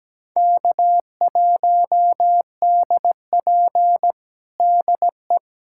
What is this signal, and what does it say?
Morse code, heard as a keyed tone, about 17 wpm, 700 Hz. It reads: K1DP DE
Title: USA Callsign DE Callsign - 4 Characters - 17wpm